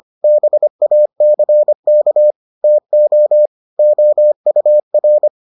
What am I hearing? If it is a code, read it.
Morse code, heard as a keyed tone, about 25 wpm, 600 Hz. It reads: BACK TO OUR